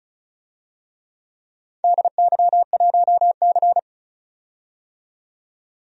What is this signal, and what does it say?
Morse code, heard as a keyed tone, about 36 wpm, 700 Hz. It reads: DY1C
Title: DXCC-100 Call Signs - 4 Characters - 35wpm